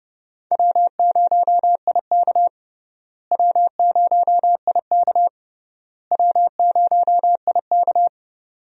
Morse code, heard as a keyed tone, about 30 wpm, 700 Hz. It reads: W0IK W0IK W0IK
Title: USA Call Signs - 4 Characters - Repeated 3x - 30wpm